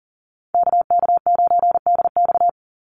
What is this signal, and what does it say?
Morse code, heard as a keyed tone, about 40 wpm, 700 Hz. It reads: KK9DX
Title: Call Signs Repeated 2x - 40wpm